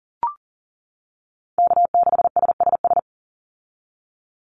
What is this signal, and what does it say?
Morse code, heard as a keyed tone, about 40 wpm, 700 Hz. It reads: K6SSS
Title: Call Signs - 40wpm